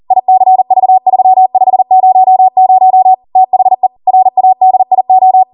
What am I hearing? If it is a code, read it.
Morse code, heard as a keyed tone, about 40 wpm, 750 Hz. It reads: ICV3500 THE RADIO
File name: Morse / 40WPM